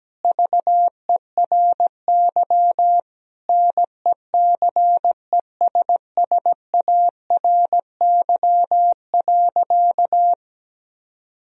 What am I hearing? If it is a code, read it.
Morse code, heard as a keyed tone, about 17 wpm, 700 Hz. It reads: VERY NECESSARY.
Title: Sentences from Top 1000 Words - Review Part 3 - 17wpm